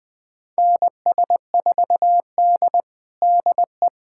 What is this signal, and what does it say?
Morse code, heard as a keyed tone, about 20 wpm, 700 Hz. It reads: NS4D DE
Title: USA Callsign DE Callsign - 4 Characters - 20wpm